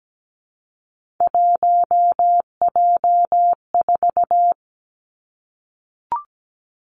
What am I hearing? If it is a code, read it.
Morse code, heard as a keyed tone, about 17 wpm, 700 Hz. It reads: 1J4
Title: Three Challenging Letter-Number Combinations 17wpm